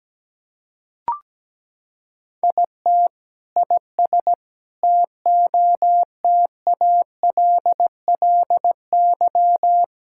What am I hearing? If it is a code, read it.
Morse code, heard as a keyed tone, about 17 wpm, 700 Hz. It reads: IT IS TOTALLY
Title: Sentences from Top 1600 Words 17wpm